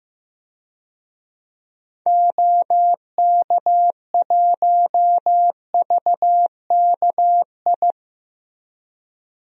Morse code, heard as a keyed tone, about 15 wpm, 700 Hz. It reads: OK1VKI
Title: DXCC-100 Call Signs - 6 Characters - Repeated 2x - 15wpm